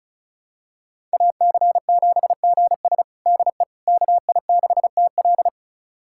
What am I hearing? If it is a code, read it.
Morse code, heard as a keyed tone, about 36 wpm, 700 Hz. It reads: AC7GS DE KI6TL
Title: USA Callsign DE Callsign - 5 Characters - 35wpm